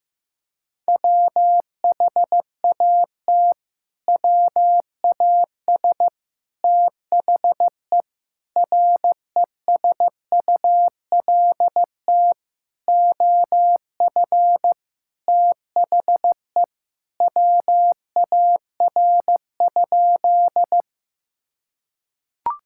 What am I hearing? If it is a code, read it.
Morse code, heard as a keyed tone, about 15 wpm, 700 Hz. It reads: WHAT WAS THE RESULT OF THE WAR?
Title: Sentences from Top 1000 Words - Review Part 5 - 15wpm